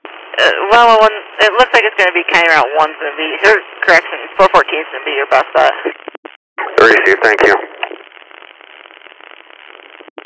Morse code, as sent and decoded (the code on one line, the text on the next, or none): none